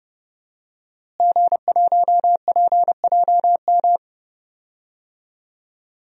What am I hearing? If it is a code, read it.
Morse code, heard as a keyed tone, about 32 wpm, 700 Hz. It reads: G1PJM
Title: DXCC-100 Call Signs - 5 Characters - Repeated 3x - 30wpm